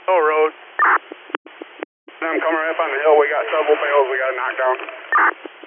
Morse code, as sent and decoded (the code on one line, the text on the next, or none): none